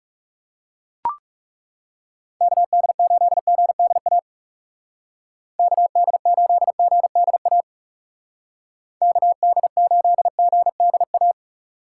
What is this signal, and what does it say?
Morse code, heard as a keyed tone, about 39 wpm, 700 Hz. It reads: KD8GDA KD8GDA KD8GDA